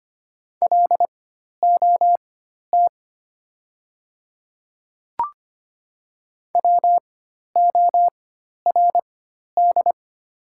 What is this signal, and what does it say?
Morse code, heard as a keyed tone, about 25 wpm, spaced as for 12 wpm, 700 Hz. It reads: LOT WORD
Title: 4x Character Spacing - Top 500 Words - 25wpm